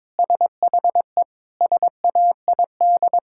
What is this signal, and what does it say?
Morse code, heard as a keyed tone, about 22 wpm, 700 Hz. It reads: SHE SAID
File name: practice / 22wpm / Sentences with 4-Word Phrases - 22wpm